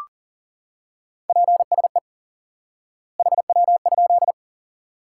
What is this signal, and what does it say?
Morse code, heard as a keyed tone, about 40 wpm, 700 Hz. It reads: PSE HW?
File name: IQE Course 3 - Lesson 89 - YAGI - Two Elements - 40wpm